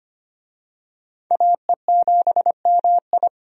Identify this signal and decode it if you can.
Morse code, heard as a keyed tone, about 25 wpm, 700 Hz. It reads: AE7MI
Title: USA Call Signs - 5 Characters - 25wpm